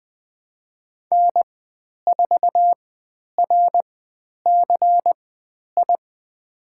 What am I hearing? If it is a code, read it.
Morse code, heard as a keyed tone, about 20 wpm, spaced as for 10 wpm, 700 Hz. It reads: N4RCI
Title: 095 Lesson 16g - Letter c - Call Signs 20wpm